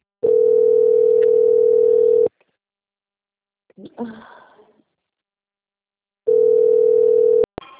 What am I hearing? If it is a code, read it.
Ringback tone (North America), 440 + 480 Hz.